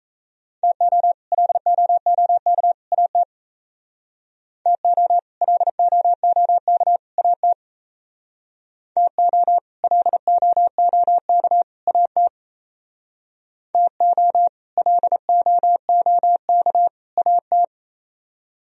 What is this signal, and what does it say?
Morse code, heard as a keyed tone, about 34 wpm, 700 Hz. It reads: TO LOOK AT TO LOOK AT TO LOOK AT TO LOOK AT
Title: Speed-Racing - Sets of 3 Words 28wpm